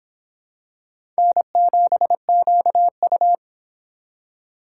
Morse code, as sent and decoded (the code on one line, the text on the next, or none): -. --... --.- ..-
N7QU